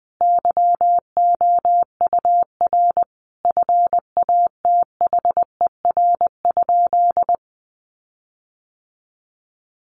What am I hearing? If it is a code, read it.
Morse code, heard as a keyed tone, about 20 wpm, 700 Hz. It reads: YOUR FATHER?
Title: Sentences from Top 1300 Words 20wpm